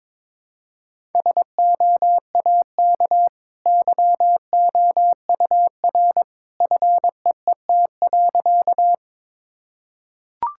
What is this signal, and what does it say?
Morse code, heard as a keyed tone, about 22 wpm, 690 Hz. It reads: SOAK YOUR FEET.
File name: practice / 22wpm / Easy Sentences with Hints - 22wpm